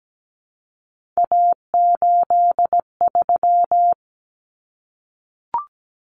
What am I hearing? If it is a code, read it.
Morse code, heard as a keyed tone, about 17 wpm, 700 Hz. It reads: A83
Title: Common Cut Numbers - 3-Digit 17wpm